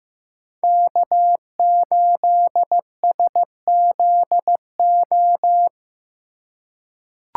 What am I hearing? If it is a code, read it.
Morse code, heard as a keyed tone, about 16 wpm, 700 Hz. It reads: K8SZO
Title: USA Call Signs - 5 Characters - 2x Character Spacing - 15wpm